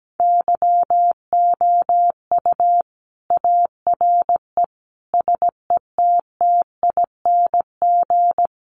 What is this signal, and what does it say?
Morse code, heard as a keyed tone, about 17 wpm, 700 Hz. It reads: YOU ARE SETTING
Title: Sentences from Top 1300 Words 17wpm